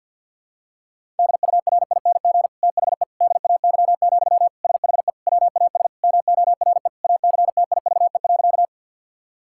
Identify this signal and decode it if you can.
Morse code, heard as a keyed tone, about 48 wpm, 700 Hz. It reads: DURING THE DAY, SHE WAS MORE ACTIVE.
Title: Sentences from Top 1000 Words - Review Part 6 - 50wpm